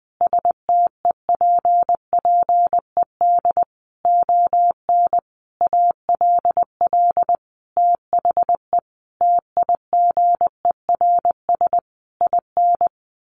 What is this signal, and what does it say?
Morse code, heard as a keyed tone, about 19 wpm, 700 Hz. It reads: STEPPED ON ALL THE TIGERS IN
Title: A Sound Of Thunder 20wpm